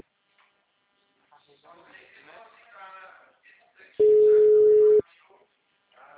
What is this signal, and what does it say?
Ringback tone (Europe, 425 Hz), with speech.